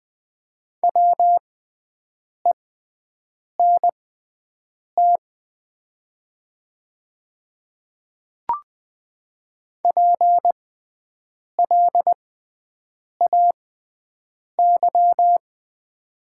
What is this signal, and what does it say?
Morse code, heard as a keyed tone, about 20 wpm, spaced as for 7 wpm, 700 Hz. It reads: WENT PLAY
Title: 6x Character Spacing - Top 200 Words - 20wpm